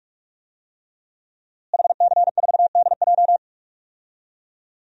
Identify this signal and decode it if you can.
Morse code, heard as a keyed tone, about 42 wpm, 700 Hz. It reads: HC4DJ
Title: DXCC-100 Call Signs - 5 Characters - Repeated 3x - 45wpm